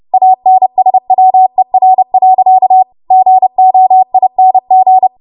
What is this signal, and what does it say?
Morse code, heard as a keyed tone, about 30 wpm, 750 Hz. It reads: ANSWER. GOING
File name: Morse / 30WPM